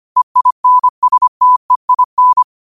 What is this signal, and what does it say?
Morse code, heard as a keyed tone, about 25 wpm, 1000 Hz. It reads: EINSTEIN
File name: qotdfiles / 25wpm